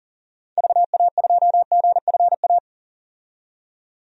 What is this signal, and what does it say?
Morse code, heard as a keyed tone, about 40 wpm, 700 Hz. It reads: VA2GFA
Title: DXCC-100 Call Signs - 6 Characters - 40wpm